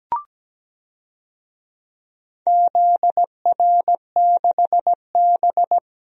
Morse code, heard as a keyed tone, about 17 wpm, 700 Hz. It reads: ZR6B
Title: DXCC-100 Call Signs - 4 Characters - 17wpm